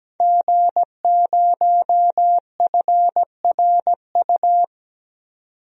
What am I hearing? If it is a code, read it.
Morse code, heard as a keyed tone, about 17 wpm, 700 Hz. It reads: G0FRU
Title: DXCC-100 Call Signs - 5 Characters - 17wpm